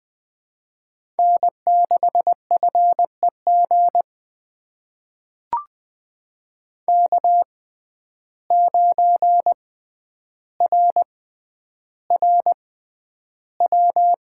Morse code, heard as a keyed tone, about 20 wpm, spaced as for 10 wpm, 700 Hz. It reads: N6FEG K9RRW
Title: USA Call Signs - 5 Characters - 6x Character Spacing - 20wpm